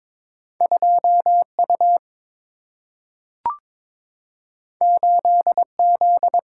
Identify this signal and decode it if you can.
Morse code, heard as a keyed tone, about 22 wpm, 700 Hz. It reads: 2U 8Z
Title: Single-Letter Single-Number 22wpm